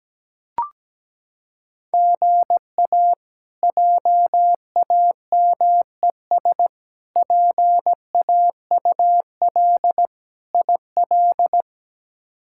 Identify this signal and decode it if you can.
Morse code, heard as a keyed tone, about 17 wpm, 700 Hz. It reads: GA JAMES PAUL IL
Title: K1USN SST QSO Practice - 17wpm